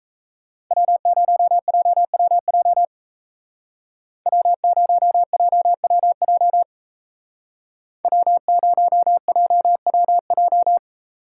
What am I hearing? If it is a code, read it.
Morse code, heard as a keyed tone, about 34 wpm, 700 Hz. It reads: W0JWJ W0JWJ W0JWJ
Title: Speed Racing - USA 1x3 Call Signs 28wpm